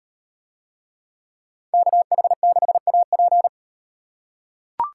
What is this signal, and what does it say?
Morse code, heard as a keyed tone, about 37 wpm, 700 Hz. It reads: KH6AP